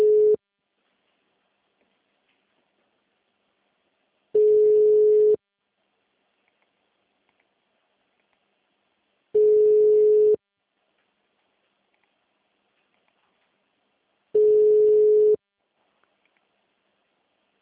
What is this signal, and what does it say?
Ringback tone (Europe, 425 Hz).